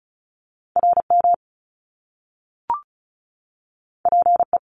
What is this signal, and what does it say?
Morse code, heard as a keyed tone, about 35 wpm, 700 Hz. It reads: RM PE
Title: Top Two Letter Combinations in Words 35wpm